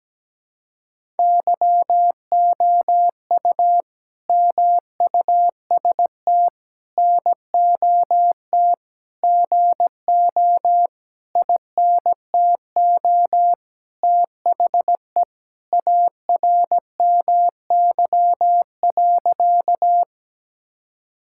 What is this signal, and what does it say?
Morse code, heard as a keyed tone, about 17 wpm, 700 Hz. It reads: YOU MUST NOT GO INTO THE ARMY.
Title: Sentences from Top 1000 Words - Review Part 5 - 17wpm